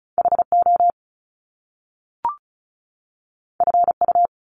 Morse code, heard as a keyed tone, about 36 wpm, 700 Hz. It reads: HO FU
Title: Top Two Letter Combinations in Words 35wpm